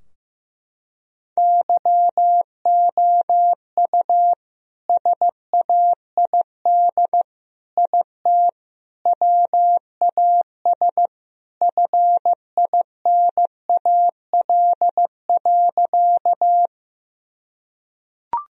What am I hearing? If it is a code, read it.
Morse code, heard as a keyed tone, about 15 wpm, 700 Hz. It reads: YOU SAID IT WAS FINAL.
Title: Sentences from Top 1000 Words - Review Part 2 - 15wpm